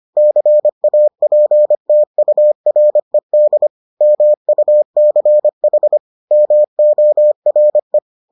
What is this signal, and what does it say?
Morse code, heard as a keyed tone, about 25 wpm, 600 Hz. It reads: CAPTURED MUCH MORE